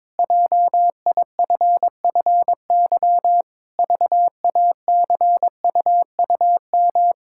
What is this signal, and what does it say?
Morse code, heard as a keyed tone, about 22 wpm, 700 Hz. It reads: JIFFY VACUUM
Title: Pangram Challenge v2 - 22wpm